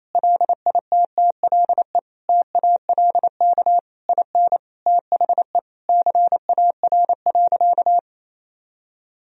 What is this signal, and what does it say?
Morse code, heard as a keyed tone, about 28 wpm, 700 Hz. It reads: LITTLE TALK IN THE CAR.
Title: Sentences from Top 1000 Words - Review Part 1 - 28wpm